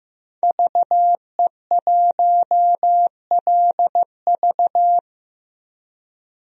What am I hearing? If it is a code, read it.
Morse code, heard as a keyed tone, about 15 wpm, 700 Hz. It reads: VE1LV
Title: DXCC-100 Call Signs - 5 Characters - 15wpm